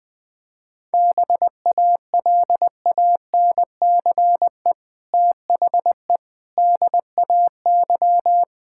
Morse code, heard as a keyed tone, about 19 wpm, 700 Hz. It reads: BALANCE THE DAY
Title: Sentences from Top 1000 Words - Review Part 6 - 20wpm